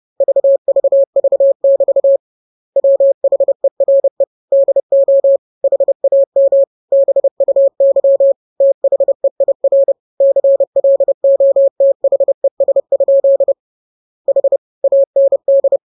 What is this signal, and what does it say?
Morse code, heard as a keyed tone, about 30 wpm, 550 Hz. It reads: VVV<BT> WHERE DO HAM BUY THEIR CLOTHES? H AND